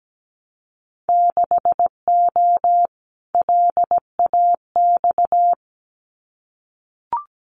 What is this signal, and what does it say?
Morse code, heard as a keyed tone, about 17 wpm, 700 Hz. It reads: 6O LAX
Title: Winter Field Day Exchanges Encore 17wpm